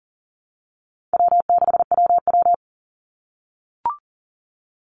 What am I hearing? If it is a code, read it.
Morse code, heard as a keyed tone, about 39 wpm, 700 Hz. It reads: W6WW
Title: NA SOTA Chaser Callsigns 40wpm